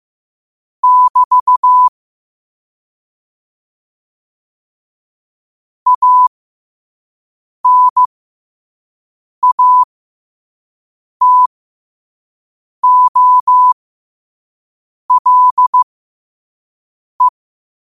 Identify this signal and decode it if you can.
Morse code, heard as a keyed tone, about 15 wpm, spaced as for 5 wpm, 1000 Hz. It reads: <BT> ANATOLE
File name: qotdfiles / 5wpm